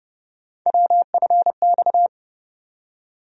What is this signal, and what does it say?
Morse code, heard as a keyed tone, about 29 wpm, 700 Hz. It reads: WFX
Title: USA Call Sign Postfixes - 3 Letters 30wpm